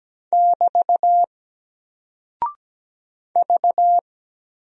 Morse code, heard as a keyed tone, about 17 wpm, 700 Hz. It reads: <BT> V
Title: Single Character (Letter:Number:Punctuation:Prosign) 17wpm